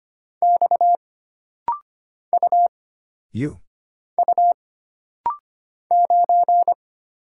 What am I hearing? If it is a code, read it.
Morse code, heard as a keyed tone, about 25 wpm, 700 Hz. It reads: X U U 9
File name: Single Character (Letter:Number:Punctuation:Prosign) - Rapid-Fire - 25wpm